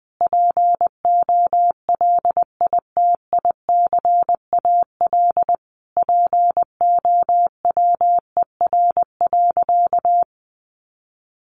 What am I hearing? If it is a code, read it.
Morse code, heard as a keyed tone, about 19 wpm, 700 Hz. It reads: POLITICAL POWER.
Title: Sentences from Top 900 Words 20wpm